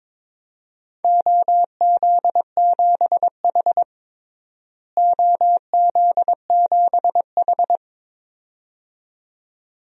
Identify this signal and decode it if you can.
Morse code, heard as a keyed tone, about 22 wpm, 700 Hz. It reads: OZ7H OZ7H
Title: DXCC-100 Call Signs - 4 Characters - Repeated 2x - 22wpm